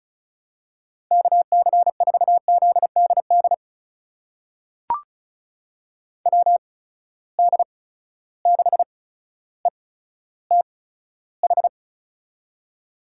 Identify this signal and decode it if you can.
Morse code, heard as a keyed tone, about 34 wpm, spaced as for 15 wpm, 700 Hz. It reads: KC4ZDD WD6ETH
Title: USA Call Signs - 6 Characters - 8x Character Spacing - 35wpm